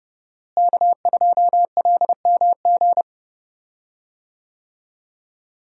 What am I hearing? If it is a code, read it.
Morse code, heard as a keyed tone, about 32 wpm, 700 Hz. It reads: K2LMG
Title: USA Call Signs - 5 Characters - Repeated 3x - 30wpm